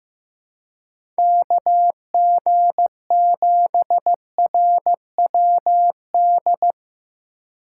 Morse code, heard as a keyed tone, about 15 wpm, 700 Hz. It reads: KG7RWD
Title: USA Call Signs - 6 Characters - 4x Character Spacing - 15wpm